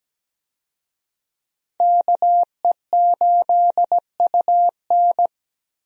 Morse code, heard as a keyed tone, about 17 wpm, 700 Hz. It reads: KE8UN
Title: USA Call Signs - 5 Characters - Repeated 2x - 17wpm